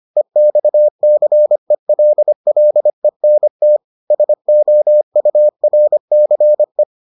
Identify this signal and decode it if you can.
Morse code, heard as a keyed tone, about 25 wpm, 600 Hz. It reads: EXCELLENT SOURCE